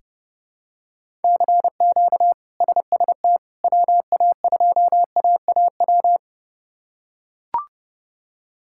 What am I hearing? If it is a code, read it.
Morse code, heard as a keyed tone, about 32 wpm, 700 Hz. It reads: CQ SST WA2AAW
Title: K1USN SST QSO Practice - 30wpm